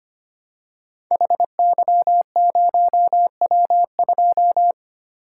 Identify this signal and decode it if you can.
Morse code, heard as a keyed tone, about 25 wpm, 700 Hz. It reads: HY0W2
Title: Code Groups - 5 Letter:Numbers - 25wpm